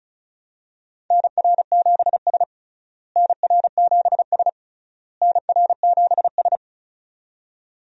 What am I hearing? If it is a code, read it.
Morse code, heard as a keyed tone, about 34 wpm, 700 Hz. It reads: NR7S NR7S NR7S
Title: USA Call Signs - 4 Characters - Repeated 3x - 35wpm